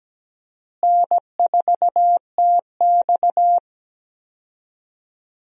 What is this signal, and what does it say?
Morse code, heard as a keyed tone, about 17 wpm, 700 Hz. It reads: N4TX
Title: USA Call Signs - 4 Characters - Repeated 3x - 17wpm